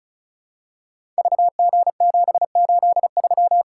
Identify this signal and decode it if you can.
Morse code, heard as a keyed tone, about 36 wpm, 700 Hz. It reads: VG783
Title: Code Groups - 5 Letter:Numbers - 35wpm